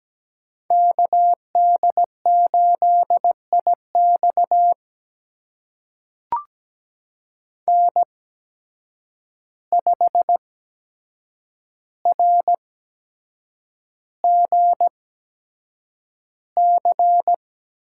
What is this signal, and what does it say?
Morse code, heard as a keyed tone, about 17 wpm, spaced as for 7 wpm, 700 Hz. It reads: KD8IX N5RGC
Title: USA Call Signs - 5 Characters - 8x Character Spacing - 17wpm